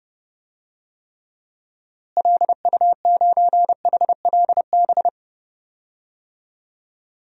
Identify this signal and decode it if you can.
Morse code, heard as a keyed tone, about 32 wpm, 700 Hz. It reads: LU9HLB
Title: DXCC-100 Call Signs - 6 Characters - Repeated 2x - 30wpm